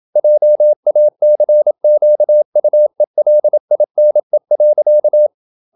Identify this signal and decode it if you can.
Morse code, heard as a keyed tone, about 27 wpm, 600 Hz. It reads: JACQUELINE.
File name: Never 0027wpm